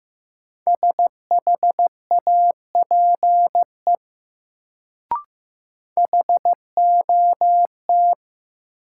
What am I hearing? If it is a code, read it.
Morse code, heard as a keyed tone, about 15 wpm, 700 Hz. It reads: SHAPE HOT
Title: Sentences from Top 500 Words - New Word Drill - 15wpm